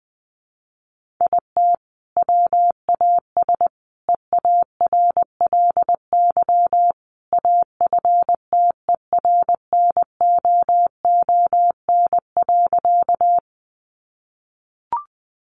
Morse code, heard as a keyed tone, about 20 wpm, 700 Hz. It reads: IT WAS EARLY AFTERNOON.
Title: Sentences from Top 1500 Words 20wpm